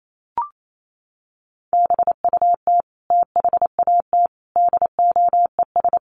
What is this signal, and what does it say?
Morse code, heard as a keyed tone, about 28 wpm, 700 Hz. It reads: BUT THAT DOES
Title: Sets of 5 Words 28wpm